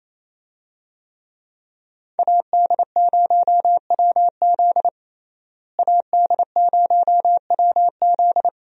Morse code, heard as a keyed tone, about 28 wpm, 700 Hz. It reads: AD0WZ AD0WZ
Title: USA Call Signs - 5 Characters - Repeated 2x - 28wpm